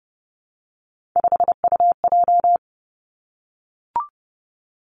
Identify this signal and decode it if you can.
Morse code, heard as a keyed tone, about 30 wpm, 700 Hz. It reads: 5UJ
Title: Three Challenging Letter-Number Combinations 30wpm